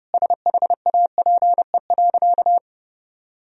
Morse code, heard as a keyed tone, about 30 wpm, 700 Hz. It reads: SHAPE.
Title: Sentences from Top 1000 Words - Review Part 3 - 30wpm